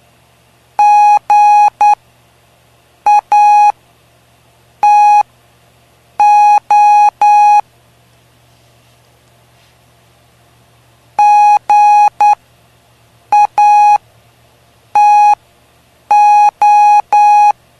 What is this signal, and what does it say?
Morse code, heard as a keyed tone, about 9 wpm, spaced as for 6 wpm, 830 Hz. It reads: GATO GATO
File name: cw / py2lcd / 05-PY2LCD-Li��o 5